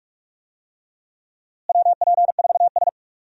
Morse code, heard as a keyed tone, about 46 wpm, 700 Hz. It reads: WP4S
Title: USA Call Signs - 4 Characters - Repeated 3x - 45wpm